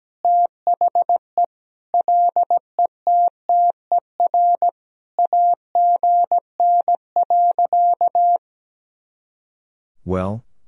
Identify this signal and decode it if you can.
Morse code, heard as a keyed tone, about 17 wpm, 700 Hz. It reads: THE LETTER AGN.
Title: Sentences from Top 300 Words and Abbreviations 17wpm